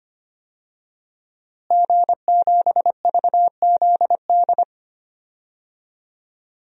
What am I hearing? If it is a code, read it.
Morse code, heard as a keyed tone, about 25 wpm, 700 Hz. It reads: G7VZD
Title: DXCC-100 Call Signs - 5 Characters - Repeated 3x - 25wpm